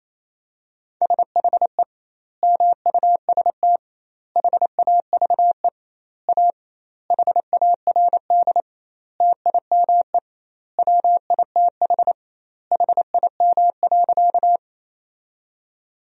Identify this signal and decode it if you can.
Morse code, heard as a keyed tone, about 28 wpm, 700 Hz. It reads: SHE MUST HAVE A HARD TIME WITH HIM.
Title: 2x Word-Spacing - Sentences from Top 300 Words - 28wpm